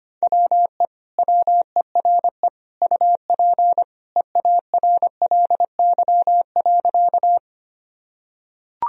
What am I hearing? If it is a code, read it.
Morse code, heard as a keyed tone, about 25 wpm, 700 Hz. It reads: WE WERE UP EARLY.
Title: Sentences from Top 1000 Words - Review Part 2 - 25wpm